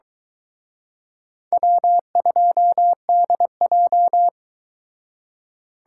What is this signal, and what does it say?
Morse code, heard as a keyed tone, about 22 wpm, 700 Hz. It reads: W2DJ